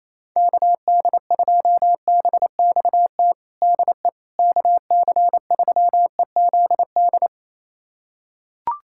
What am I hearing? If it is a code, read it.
Morse code, heard as a keyed tone, about 28 wpm, 700 Hz. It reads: KD2BXT DE KC3EZD
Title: USA Callsign DE Callsign 28wpm